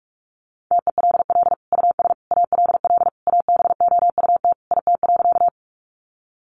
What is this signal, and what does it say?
Morse code, heard as a keyed tone, about 44 wpm, 700 Hz. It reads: TELL US ALL ABOUT IT.
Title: Sentences from Top 200 Words 45wpm